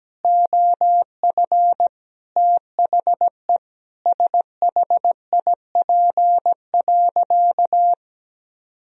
Morse code, heard as a keyed tone, about 17 wpm, 700 Hz. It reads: OF THE SHIP.
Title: Sentences from Top 1000 Words - Review Part 2 - 17wpm